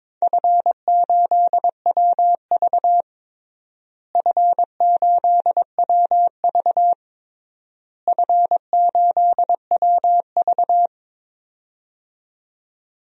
Morse code, heard as a keyed tone, about 22 wpm, 700 Hz. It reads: F8WV F8WV F8WV
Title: DXCC-100 Call Signs - 4 Characters - Repeated 3x - 22wpm